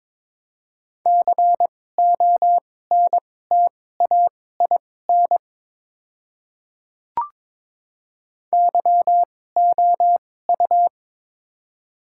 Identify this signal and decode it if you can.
Morse code, heard as a keyed tone, about 22 wpm, spaced as for 16 wpm, 700 Hz. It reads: CONTAIN YOU
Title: 2x Character Spacing - Top 500 Words - 22wpm